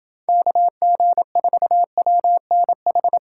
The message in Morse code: KG4WNH

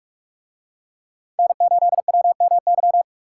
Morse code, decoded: N8WMY

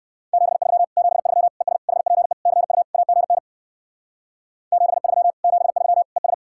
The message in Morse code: 73 73 ES HPE CU AGN 73 73 ES

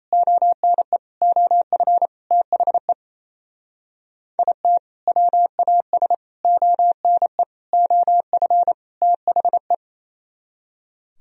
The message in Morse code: ONE OF THE IT WAS ONE OF THE

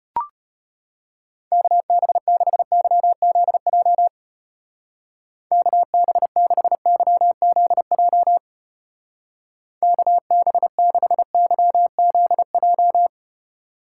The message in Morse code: KB6YZJ KB6YZJ KB6YZJ

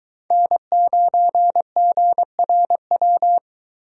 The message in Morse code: N9GRW

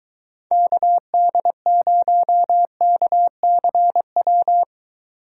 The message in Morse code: KD0KCW

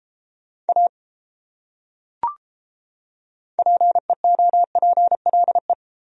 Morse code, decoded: A PEOPLE

